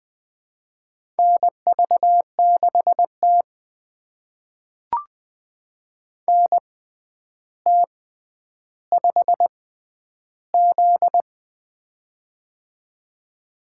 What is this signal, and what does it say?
Morse code, heard as a keyed tone, about 20 wpm, spaced as for 10 wpm, 700 Hz. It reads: NV6T NT5Z
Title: USA Call Signs - 4 Characters - 6x Character Spacing - 20wpm